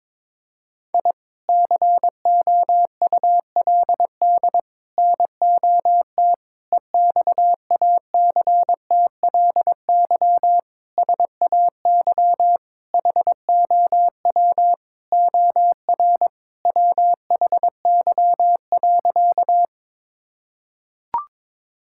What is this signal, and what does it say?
Morse code, heard as a keyed tone, about 22 wpm, 700 Hz. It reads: I COULD NOT EXACTLY SAY HOW OR WHY.
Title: Sentences from Top 1000 Words - Review Part 6 - 22wpm